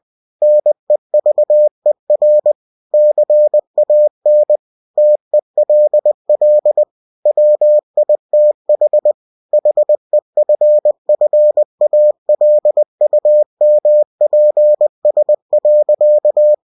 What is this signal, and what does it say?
Morse code, heard as a keyed tone, about 20 wpm, 600 Hz. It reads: NEVER CAN TELL WITH HEFFALUMPS.
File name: cw-books / winnie-the-pooh / winnie-20wpm-page-0038